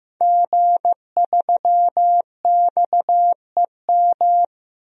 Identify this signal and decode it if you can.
Morse code, heard as a keyed tone, about 15 wpm, 700 Hz. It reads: G3XEM
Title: DXCC-100 Call Signs - 5 Characters - 15wpm